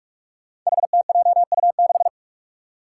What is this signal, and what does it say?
Morse code, heard as a keyed tone, about 41 wpm, 700 Hz. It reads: HTJU6
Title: Five Challenging Letter-Number Combinations 45wpm